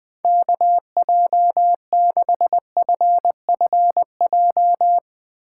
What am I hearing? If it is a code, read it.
Morse code, heard as a keyed tone, about 20 wpm, 700 Hz. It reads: KJ6FFJ
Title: Speed Racing - USA 2x3 Call Signs 17wpm